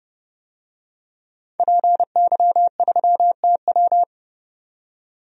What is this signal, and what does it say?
Morse code, heard as a keyed tone, about 30 wpm, 700 Hz. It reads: PY3TW
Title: DXCC-100 Call Signs - 5 Characters - 30wpm